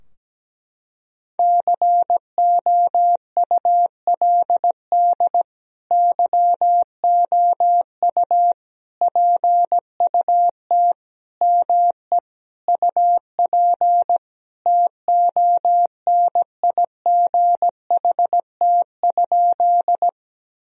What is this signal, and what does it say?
Morse code, heard as a keyed tone, about 17 wpm, 700 Hz. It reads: COULD YOU PUT ME UP TONIGHT?
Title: Sentences from Top 1000 Words - Review Part 3 - 17wpm